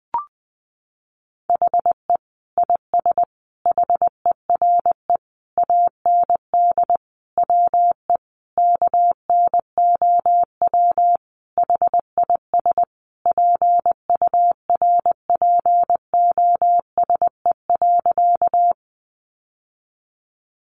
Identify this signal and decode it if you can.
Morse code, heard as a keyed tone, about 19 wpm, 700 Hz. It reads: HE IS HERE AND WE KNOW HIS PURPOSE.
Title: Sentences from Top 800 Words 20wpm